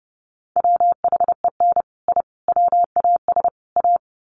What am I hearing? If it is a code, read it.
Morse code, heard as a keyed tone, about 30 wpm, 700 Hz. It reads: WHEN I WAS A
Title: Sets of 4 Words 30wpm